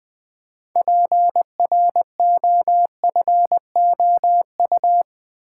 Morse code, heard as a keyed tone, about 20 wpm, 700 Hz. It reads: PROFOU
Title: Build A Word - 3 Letters At A Time - 8-Letter Words - 20wpm